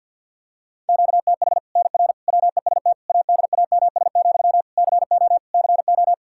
Morse code, heard as a keyed tone, about 53 wpm, 700 Hz. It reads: QTH NR WEST ADAMS, CO CO
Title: Practice QSO - Exchange 1 - Beginning - 50wpm